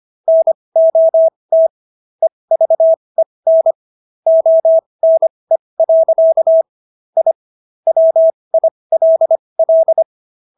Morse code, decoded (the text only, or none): NOT EVEN ONE. I WILL